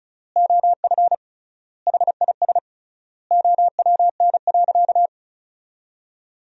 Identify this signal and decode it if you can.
Morse code, heard as a keyed tone, about 36 wpm, 700 Hz. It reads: OF HIS OWN.
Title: 3x Word-Spacing - Sentences from Top 300 Words - 35wpm